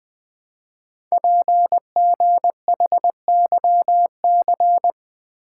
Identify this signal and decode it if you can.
Morse code, heard as a keyed tone, about 20 wpm, 700 Hz. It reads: PGHYC
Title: Code Groups - 5 Letters - 20wpm